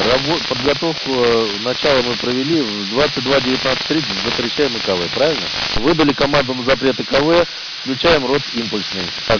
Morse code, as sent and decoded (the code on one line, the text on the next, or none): none